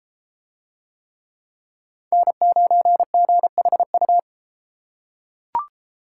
Morse code, heard as a keyed tone, about 32 wpm, 700 Hz. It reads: N9GHU